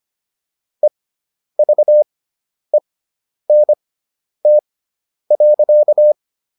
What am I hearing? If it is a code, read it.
Morse code, heard as a keyed tone, about 25 wpm, spaced as for 10 wpm, 600 Hz. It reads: EVENT.